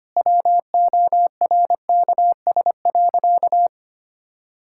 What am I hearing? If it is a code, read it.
Morse code, heard as a keyed tone, about 25 wpm, 700 Hz. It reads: WORKS.
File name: practice / 25wpm / Sentences with 2-Word Phrases - 25wpm